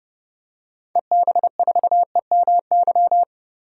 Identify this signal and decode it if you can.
Morse code, heard as a keyed tone, about 30 wpm, 700 Hz. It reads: EB4EMY
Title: DXCC-100 Call Signs - 6 Characters - 30wpm